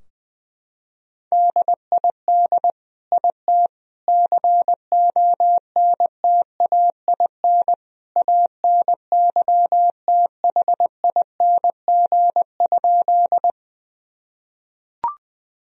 Morse code, heard as a keyed tone, about 20 wpm, 700 Hz. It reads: DID IT CONTAIN ANYTHING?
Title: Sentences from Top 1000 Words - Review Part 2 - 20wpm